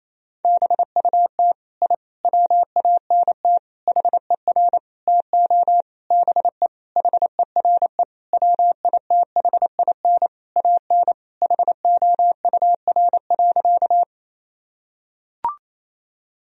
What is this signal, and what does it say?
Morse code, heard as a keyed tone, about 28 wpm, 700 Hz. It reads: BUT I WANT HER TO BE HERE WITHIN AN HOUR.
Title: Sentences from Top 1000 Words - Review Part 4 - 28wpm